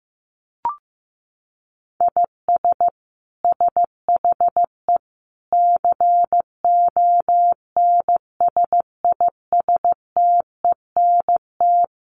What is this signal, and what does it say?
Morse code, heard as a keyed tone, about 15 wpm, 700 Hz. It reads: IS SHE CONSISTENT
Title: Sentences from Top 1500 Words 15wpm